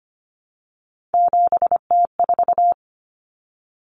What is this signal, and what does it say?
Morse code, heard as a keyed tone, about 25 wpm, 700 Hz. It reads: 7T4
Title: Common Cut Numbers - 3-Digit 25wpm